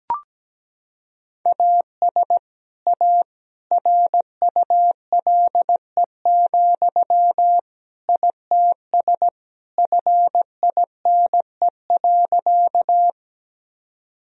Morse code, heard as a keyed tone, about 17 wpm, 700 Hz. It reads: AS A RULE, ITS FINE.
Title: Sentences from Top 1000 Words - Review Part 2 - 17wpm